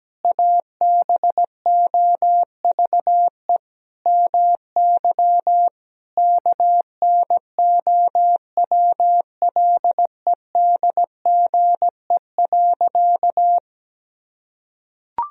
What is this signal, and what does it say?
Morse code, heard as a keyed tone, about 17 wpm, 700 Hz. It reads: ABOVE MY KNOWLEDGE.
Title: Sentences from Top 1000 Words - Review Part 3 - 17wpm